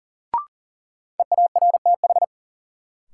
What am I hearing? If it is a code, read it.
Morse code, heard as a keyed tone, about 40 wpm, 700 Hz. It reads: EARTH
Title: Sentences from Top 200 Words - New Word Drill - 40wpm